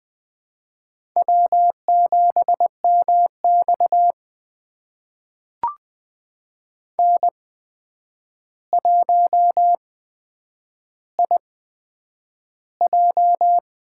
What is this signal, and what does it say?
Morse code, heard as a keyed tone, about 20 wpm, spaced as for 9 wpm, 700 Hz. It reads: W7MX N1IJ
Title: USA Call Signs - 4 Characters - 8x Character Spacing - 20wpm